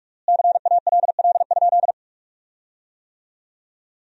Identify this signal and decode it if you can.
Morse code, heard as a keyed tone, about 41 wpm, 700 Hz. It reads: CALL?
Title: 3x Word-Spacing - Sentences from Top 100 Words - 45wpm